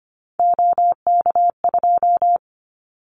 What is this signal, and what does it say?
Morse code, heard as a keyed tone, about 25 wpm, 690 Hz. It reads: OK2
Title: Callsign Prefixes for Top 50 DXCC - 25wpm